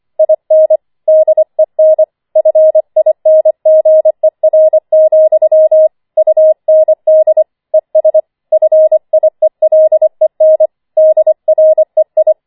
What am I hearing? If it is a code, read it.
Morse code, heard as a keyed tone, about 25 wpm, 630 Hz. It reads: IN DEN FINGER, UND ES FIELEN DREI